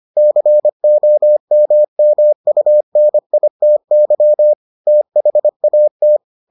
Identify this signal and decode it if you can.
Morse code, heard as a keyed tone, about 25 wpm, 600 Hz. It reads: COMMUNITY THAT